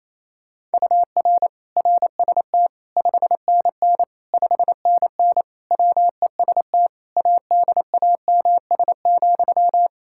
Morse code, heard as a keyed tone, about 28 wpm, 700 Hz. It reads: UR RST 5NN 5NN WEST ADAMS,
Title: Practice QSO - Exchange 1 - Beginning - 28wpm